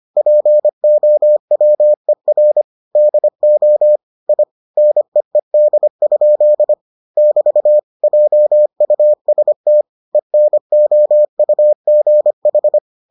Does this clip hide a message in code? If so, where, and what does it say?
Yes: Morse code from 0.2 to 12.8 s, heard as a keyed tone, about 25 wpm, 600 Hz. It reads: POWER DO I NEED? <BT> JUST ENOUGH